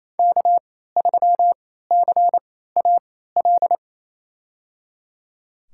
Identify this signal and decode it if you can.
Morse code, heard as a keyed tone, about 28 wpm, spaced as for 16 wpm, 700 Hz. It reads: K3CAL
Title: USA Call Signs - 3x Character Spacing - 28wpm